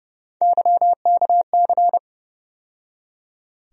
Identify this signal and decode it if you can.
Morse code, heard as a keyed tone, about 30 wpm, 700 Hz. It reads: YKC
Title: USA Call Sign Postfixes - 3 Letters 30wpm